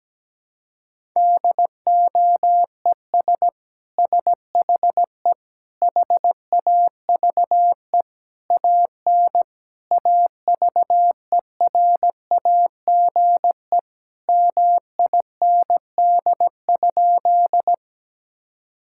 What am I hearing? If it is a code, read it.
Morse code, heard as a keyed tone, about 17 wpm, 700 Hz. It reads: DOES SHE HAVE AN AVERAGE MIND?
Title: Sentences from Top 1000 Words - Review Part 6 - 17wpm